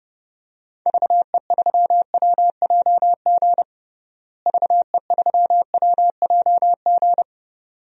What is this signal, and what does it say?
Morse code, heard as a keyed tone, about 30 wpm, 700 Hz. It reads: VE3WJG VE3WJG
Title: DXCC-100 Call Signs - 6 Characters - Repeated 2x - 30wpm